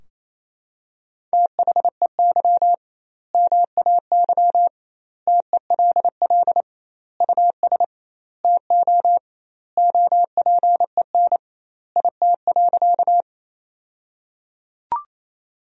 Morse code, heard as a keyed tone, about 28 wpm, 700 Hz. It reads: THEY MAY TELL US TO OPEN IT.
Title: 2x Word-Spacing - Sentences from Top 300 Words - 28wpm